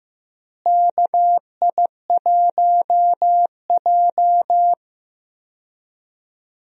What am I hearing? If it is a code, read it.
Morse code, heard as a keyed tone, about 15 wpm, 700 Hz. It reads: KI1J
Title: USA Call Signs - 4 Characters - Repeated 2x - 15wpm